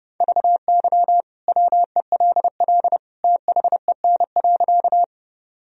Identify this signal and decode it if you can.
Morse code, heard as a keyed tone, about 30 wpm, 700 Hz. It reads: VY WELL THEN.
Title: Sentences from Top 200 Words and Abbreviations 30wpm